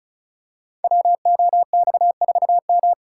Morse code, heard as a keyed tone, about 36 wpm, 700 Hz. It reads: WOX4M
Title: Code Groups - 5 Letter:Numbers - 35wpm